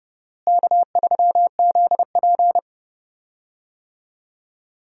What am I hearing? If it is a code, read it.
Morse code, heard as a keyed tone, about 32 wpm, 700 Hz. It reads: K3ZP
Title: USA Call Signs - 4 Characters - Repeated 2x - 30wpm